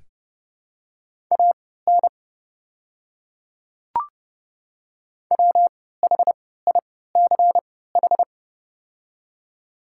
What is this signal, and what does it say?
Morse code, heard as a keyed tone, about 32 wpm, spaced as for 17 wpm, 700 Hz. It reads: AN WHICH
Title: 3x Character Spacing - Top 100 Words - 30wpm